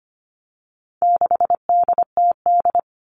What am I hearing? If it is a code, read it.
Morse code, heard as a keyed tone, about 25 wpm, 700 Hz. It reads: 6DTD
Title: All Cut Numbers - 4-Digit 25wpm